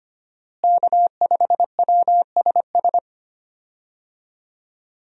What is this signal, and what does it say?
Morse code, heard as a keyed tone, about 25 wpm, 700 Hz. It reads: K5WSS
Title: USA Call Signs - 5 Characters - Repeated 2x - 25wpm